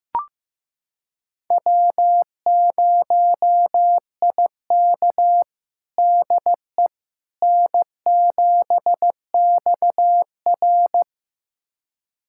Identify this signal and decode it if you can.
Morse code, heard as a keyed tone, about 15 wpm, 700 Hz. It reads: W0IK DE N7XR
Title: USA Callsign DE Callsign - 4 Characters - 15wpm